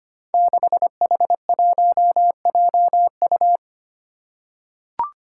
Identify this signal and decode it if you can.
Morse code, heard as a keyed tone, about 25 wpm, 700 Hz. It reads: 6H1JU
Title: Five Challenging Letter-Number Combinations 25wpm